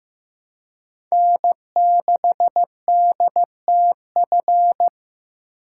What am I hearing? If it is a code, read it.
Morse code, heard as a keyed tone, about 15 wpm, 700 Hz. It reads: N6DTF
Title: USA Call Signs - 2x Character Spacing - 15wpm